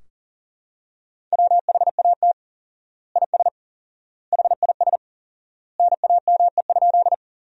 Morse code, heard as a keyed tone, about 40 wpm, 700 Hz. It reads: WHAT IS HIS NAME?
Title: 4x Word-Spacing - Sentences from Top 200 Words - 40wpm